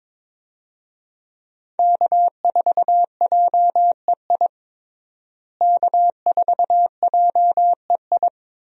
Morse code, heard as a keyed tone, about 22 wpm, 700 Hz. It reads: K4JEI K4JEI
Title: USA Call Signs - 5 Characters - Repeated 2x - 22wpm